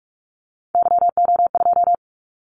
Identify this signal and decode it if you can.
Morse code, heard as a keyed tone, about 46 wpm, 700 Hz. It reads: YO2
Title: Callsign Prefixes for Top 25 DXCC - 45wpm